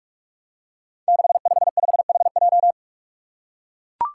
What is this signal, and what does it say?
Morse code, heard as a keyed tone, about 44 wpm, 700 Hz. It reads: 655HJ
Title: Five Challenging Letter-Number Combinations 45wpm